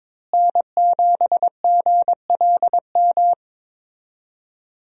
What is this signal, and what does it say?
Morse code, heard as a keyed tone, about 21 wpm, 700 Hz. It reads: N7GLM
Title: USA Call Signs - 5 Characters - Repeated 2x - 22wpm